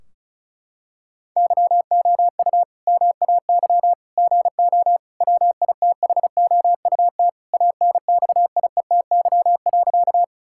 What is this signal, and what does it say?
Morse code, heard as a keyed tone, about 34 wpm, 700 Hz. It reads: YOU MAY GO WITHOUT ANXIETY.
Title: Sentences from Top 1600 Words 35wpm